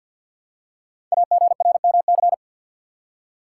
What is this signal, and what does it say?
Morse code, heard as a keyed tone, about 46 wpm, 700 Hz. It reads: AGRMC
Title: Code Groups - 5 Letters - 50wpm